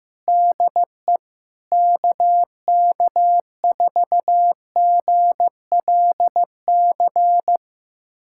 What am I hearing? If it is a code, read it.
Morse code, heard as a keyed tone, about 15 wpm, 700 Hz. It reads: DE KK4GLC
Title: USA Callsign DE Callsign 15wpm